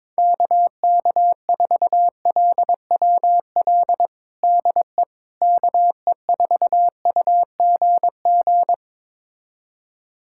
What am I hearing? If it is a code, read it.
Morse code, heard as a keyed tone, about 22 wpm, 700 Hz. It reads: KK4LWL DE KE4UGG
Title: USA Callsign DE Callsign - 6 Characters - 22wpm